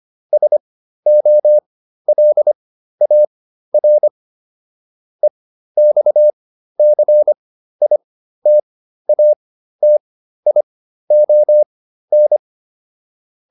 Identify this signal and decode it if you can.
Morse code, heard as a keyed tone, about 25 wpm, spaced as for 13 wpm, 600 Hz. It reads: SOLAR EXCITATION
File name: SS211 13